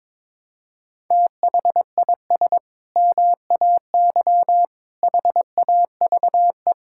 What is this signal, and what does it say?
Morse code, heard as a keyed tone, about 22 wpm, 700 Hz. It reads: THIS MAY HAVE
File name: practice / 22wpm / Sentences with 8-Word Phrases - 22wpm